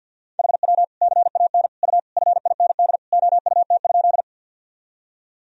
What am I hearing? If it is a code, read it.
Morse code, heard as a keyed tone, about 50 wpm, 700 Hz. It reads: HW CAN U FIND OUT?
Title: Sentences from Top 200 Words and Abbreviations 50wpm